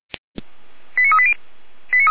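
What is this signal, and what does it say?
Five-tone selective call (ZVEI), heard as quick runs of five tones, tones of about 70 ms. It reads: 89380
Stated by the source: Five-tone ZVEI call received by SDR, from the feuerwehr-orlamuende.org fire brigade audio server.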